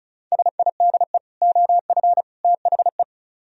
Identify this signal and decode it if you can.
Morse code, heard as a keyed tone, about 42 wpm, 700 Hz. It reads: SIDE OF THE
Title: Speed-Racing - Sets of 3 Words 35wpm